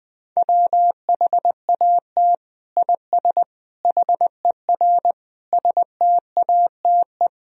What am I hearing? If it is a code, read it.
Morse code, heard as a keyed tone, about 20 wpm, 700 Hz. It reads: WHAT IS HER STATE
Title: Sentences from Top 1000 Words - Review Part 2 - 20wpm